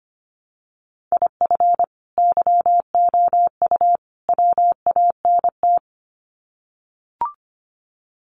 Morse code, heard as a keyed tone, about 25 wpm, 700 Hz. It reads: IF YOU WANT
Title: Sets of 3 Words 25wpm